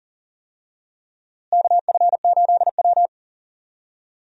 Morse code, heard as a keyed tone, about 40 wpm, 700 Hz. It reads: KF8W
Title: USA Call Signs - 4 Characters - Repeated 3x - 40wpm